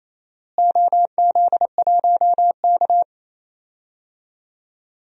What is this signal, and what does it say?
Morse code, heard as a keyed tone, about 28 wpm, 700 Hz. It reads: OZ1K
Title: DXCC-100 Call Signs - 4 Characters - Repeated 2x - 28wpm